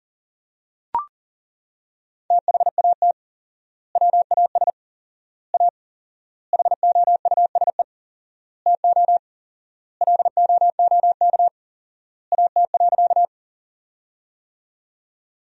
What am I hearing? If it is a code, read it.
Morse code, heard as a keyed tone, about 38 wpm, 700 Hz. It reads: THAT WAS A HOUSE TO LOOK AT.
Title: 4x Word-Spacing - Sentences from Top 200 Words - 40wpm